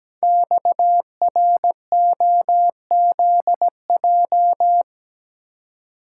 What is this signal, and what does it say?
Morse code, heard as a keyed tone, about 17 wpm, 700 Hz. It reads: XROZJ
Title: Code Groups - 5 Letters - 17wpm